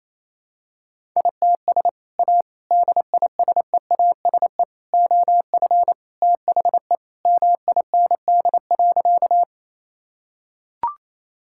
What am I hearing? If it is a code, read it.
Morse code, heard as a keyed tone, about 28 wpm, 700 Hz. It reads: ITS A DISEASE OF THE MIND.
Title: Sentences from Top 1000 Words - Review Part 6 - 28wpm